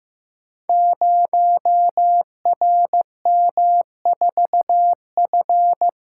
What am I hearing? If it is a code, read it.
Morse code, heard as a keyed tone, about 15 wpm, 700 Hz. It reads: 0RM4F
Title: Code Groups - 5 Letter:Numbers - 15wpm